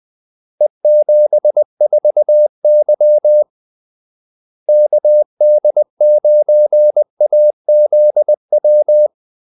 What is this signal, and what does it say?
Morse code, heard as a keyed tone, about 20 wpm, 600 Hz. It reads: E74Y KD9AZW
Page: Kaldesignaler 20 wpm